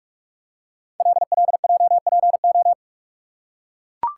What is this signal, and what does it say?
Morse code, heard as a keyed tone, about 44 wpm, 700 Hz. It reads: LLJPO
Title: Code Groups - 5 Letters - 45wpm